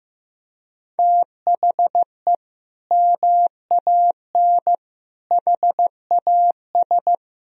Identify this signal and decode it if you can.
Morse code, heard as a keyed tone, about 16 wpm, 700 Hz. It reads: THE MAN HAS
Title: Sentences from Top 1000 Words - Review Part 3 - 15wpm